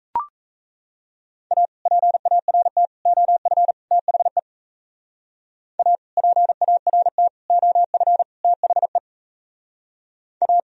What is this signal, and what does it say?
Morse code, heard as a keyed tone, about 37 wpm, 700 Hz. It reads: A PART OF THE A PART OF THE A